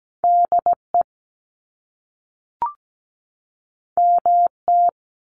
Morse code, heard as a keyed tone, about 17 wpm, 700 Hz. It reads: DE MT
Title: US States 17wpm